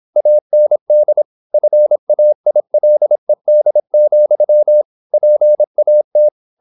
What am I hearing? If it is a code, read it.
Morse code, heard as a keyed tone, about 26 wpm, 600 Hz. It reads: AND FAILED, PAT
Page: Never 26wpm | CW med Gnister